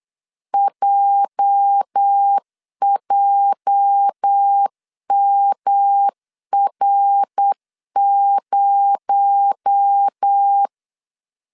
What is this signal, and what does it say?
Morse code, heard as a keyed tone, about 8 wpm, 800 Hz. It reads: JJMR0
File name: CW / DET / 1 9 1500C S C 8wpm 5653